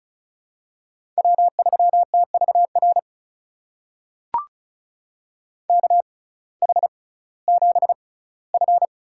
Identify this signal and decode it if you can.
Morse code, heard as a keyed tone, about 36 wpm, spaced as for 19 wpm, 700 Hz. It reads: W3TVR KH7F
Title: USA Call Signs - 5 Characters - 6x Character Spacing - 35wpm